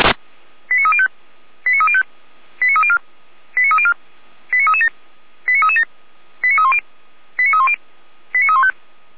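Five-tone selective call (ZVEI), heard as quick runs of five tones, tones of about 70 ms. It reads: 89305 89305 89304 89304 89337 89337 89310 89310 89315
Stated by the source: ZVEI five-tone selective call 89104, received by SDR, from the server of feuerwehr-orlamuende (fire brigade Orlamünde).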